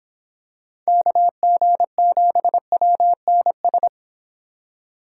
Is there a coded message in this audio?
Yes: Morse code from 0.9 to 3.9 s, heard as a keyed tone, about 26 wpm, 700 Hz. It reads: KG7WNS